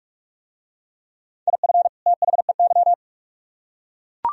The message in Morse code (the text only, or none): IF THEY